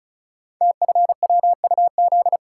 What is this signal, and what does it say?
Morse code, heard as a keyed tone, about 36 wpm, 700 Hz. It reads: TFWUZ
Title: Code Groups - 5 Letter:Numbers - 35wpm